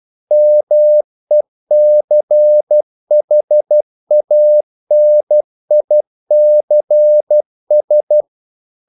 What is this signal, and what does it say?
Morse code, heard as a keyed tone, about 12 wpm, 600 Hz. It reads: MECHANICS
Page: English quotations in CW (telegraphy - morse)